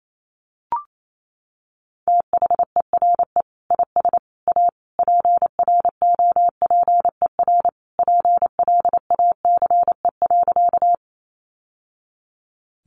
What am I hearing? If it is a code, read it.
Morse code, heard as a keyed tone, about 28 wpm, 700 Hz. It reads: THERE IS A PROPER PLACE.
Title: Sentences from Top 800 Words 28wpm